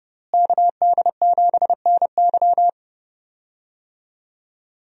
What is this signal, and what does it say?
Morse code, heard as a keyed tone, about 30 wpm, 700 Hz. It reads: KD7NY
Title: USA Call Signs - 5 Characters - Repeated 3x - 30wpm